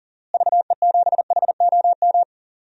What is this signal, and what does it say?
Morse code, heard as a keyed tone, about 40 wpm, 700 Hz. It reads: VE7HOM
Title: DXCC-100 Call Signs - 6 Characters - Repeated 3x - 40wpm